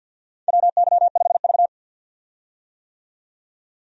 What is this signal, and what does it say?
Morse code, heard as a keyed tone, about 51 wpm, 700 Hz. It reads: WY5V
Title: USA Call Signs - 4 Characters - 50wpm